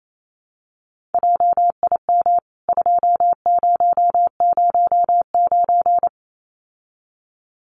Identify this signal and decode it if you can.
Morse code, heard as a keyed tone, about 28 wpm, 700 Hz. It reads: JIM 2009
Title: CWT Exchange 28wpm